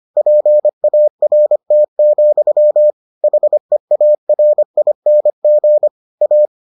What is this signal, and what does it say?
Morse code, heard as a keyed tone, about 25 wpm, 600 Hz. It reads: PART, HEARING A